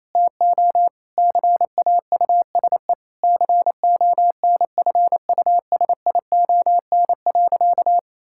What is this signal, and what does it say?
Morse code, heard as a keyed tone, about 28 wpm, 700 Hz. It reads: TO CAUSE CONFUSION.
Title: Sentences from Top 1600 Words 28wpm